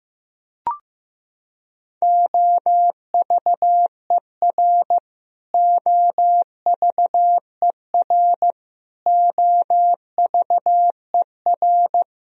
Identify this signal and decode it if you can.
Morse code, heard as a keyed tone, about 15 wpm, 700 Hz. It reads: OVER OVER OVER
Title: Top 100 Words Repeated 15wpm